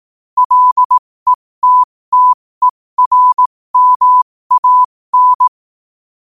Morse code, heard as a keyed tone, about 18 wpm, spaced as for 15 wpm, 1000 Hz. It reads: LETTERMAN
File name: qotdfiles / 15wpm